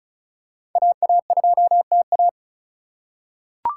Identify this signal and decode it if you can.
Morse code, heard as a keyed tone, about 36 wpm, 700 Hz. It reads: AA2TA
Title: USA Call Signs - 3x Character Spacing - 35wpm